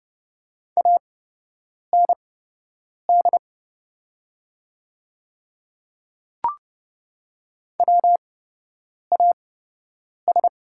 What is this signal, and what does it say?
Morse code, heard as a keyed tone, about 30 wpm, spaced as for 8 wpm, 700 Hz. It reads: AND WAS
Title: 8x Character Spacing - Top 200 Words - 30wpm